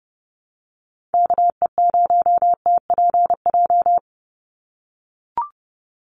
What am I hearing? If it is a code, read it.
Morse code, heard as a keyed tone, about 30 wpm, 700 Hz. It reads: KE0TPJ
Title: USA Call Signs 30wpm